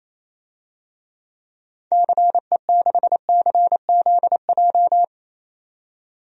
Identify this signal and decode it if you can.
Morse code, heard as a keyed tone, about 28 wpm, 700 Hz. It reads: CE6CZJ
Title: DXCC-100 Call Signs - 6 Characters - 28wpm